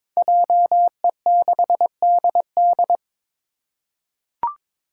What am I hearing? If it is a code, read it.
Morse code, heard as a keyed tone, about 22 wpm, 700 Hz. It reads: JE6DD
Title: Five Challenging Letter-Number Combinations 22wpm